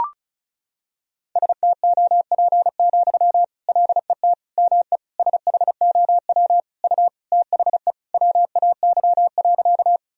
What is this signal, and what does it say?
Morse code, heard as a keyed tone, about 34 wpm, 700 Hz. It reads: STOP, LET ME SHOW U THE WAY.
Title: Sentences from Top 300 Words and Abbreviations 35wpm